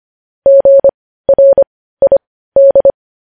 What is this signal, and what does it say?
Morse code, heard as a keyed tone, about 25 wpm, spaced as for 15 wpm, 550 Hz. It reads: GRID
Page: news-headlines-in-morse-code-at-15-wpm